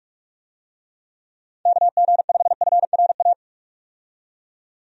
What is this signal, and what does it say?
Morse code, heard as a keyed tone, about 44 wpm, 700 Hz. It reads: KG5FRA
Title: USA Call Signs - 6 Characters - 45wpm